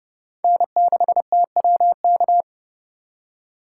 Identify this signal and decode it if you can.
Morse code, heard as a keyed tone, about 38 wpm, 700 Hz. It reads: N6TWK